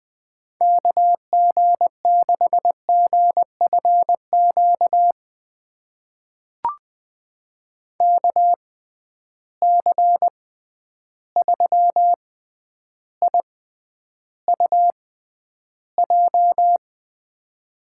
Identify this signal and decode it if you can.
Morse code, heard as a keyed tone, about 20 wpm, spaced as for 10 wpm, 700 Hz. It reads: KG6GFQ KC3IUJ
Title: USA Call Signs - 6 Characters - 6x Character Spacing - 20wpm